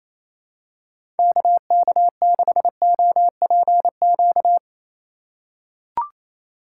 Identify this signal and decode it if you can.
Morse code, heard as a keyed tone, about 28 wpm, 700 Hz. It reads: KK6OPQ
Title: USA Call Signs - 6 Characters - 3x Character Spacing - 28wpm